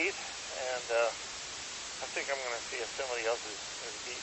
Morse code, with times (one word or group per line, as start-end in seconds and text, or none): none